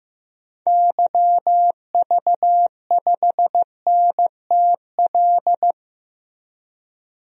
Morse code, heard as a keyed tone, about 15 wpm, 700 Hz. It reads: YV5NTL
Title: DXCC-100 Call Signs - 6 Characters - 15wpm